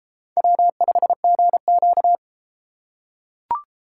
Morse code, heard as a keyed tone, about 31 wpm, 700 Hz. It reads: W5GQ